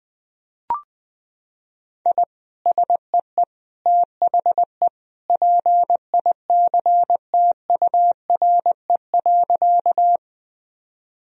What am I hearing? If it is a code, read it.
Morse code, heard as a keyed tone, about 21 wpm, 700 Hz. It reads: I SEE THE PICTURE.
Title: Sentences from Top 1000 Words - Review Part 1 - 20wpm